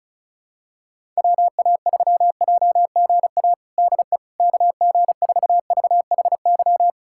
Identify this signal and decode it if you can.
Morse code, heard as a keyed tone, about 34 wpm, 700 Hz. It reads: WA3JGA DE KG4VHY
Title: USA Callsign DE Callsign 35wpm